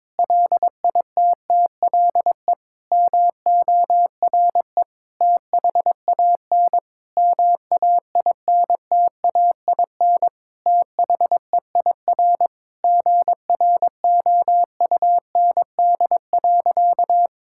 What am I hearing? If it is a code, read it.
Morse code, heard as a keyed tone, about 22 wpm, 700 Hz. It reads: LITTLE MORE THAN MAINTAIN THEIR GROUND.
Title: Sentences from Top 1000 Words - Review Part 5 - 22wpm